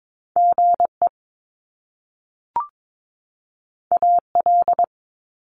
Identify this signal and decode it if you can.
Morse code, heard as a keyed tone, about 22 wpm, 700 Hz. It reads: GE AL
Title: Top Two Letter Combinations in Words 22wpm